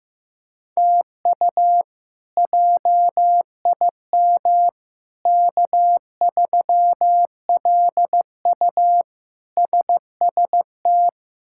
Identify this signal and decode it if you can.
Morse code, heard as a keyed tone, about 15 wpm, 700 Hz. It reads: TU JIM K3LU SST
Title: K1USN SST QSO Practice - 15wpm